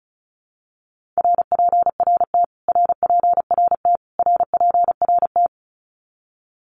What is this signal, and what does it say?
Morse code, heard as a keyed tone, about 34 wpm, 700 Hz. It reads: RPRT RPRT RPRT
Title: Abbreviations Repeated 3x - 35wpm